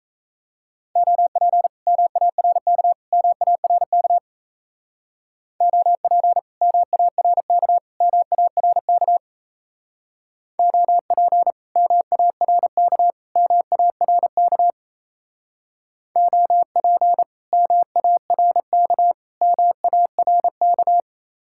Morse code, heard as a keyed tone, about 36 wpm, 700 Hz. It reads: OP MARK MARK OP MARK MARK OP MARK MARK OP MARK MARK